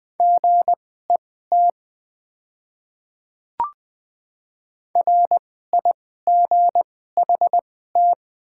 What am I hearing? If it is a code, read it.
Morse code, heard as a keyed tone, about 20 wpm, spaced as for 14 wpm, 700 Hz. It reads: GET RIGHT
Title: 2x Character Spacing - Top 300 Words - 20wpm